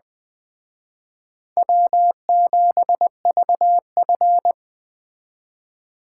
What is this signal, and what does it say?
Morse code, heard as a keyed tone, about 21 wpm, 700 Hz. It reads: W7VF